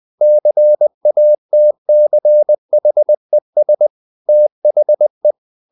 Morse code, heard as a keyed tone, about 20 wpm, 600 Hz. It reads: CATCHES THE
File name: CW / Practice / EnglishSayings20